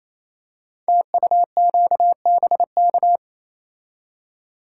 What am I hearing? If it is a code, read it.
Morse code, heard as a keyed tone, about 27 wpm, 700 Hz. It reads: TUQBK
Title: Code Groups - 5 Letters - 28wpm